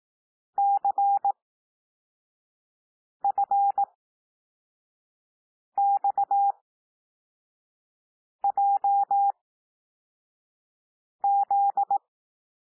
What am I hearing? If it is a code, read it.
Morse code, heard as a keyed tone, about 18 wpm, spaced as for 4 wpm, 800 Hz. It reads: CFXJZ
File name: cwmp3 / L2 SOUND LETTERS 18 wpm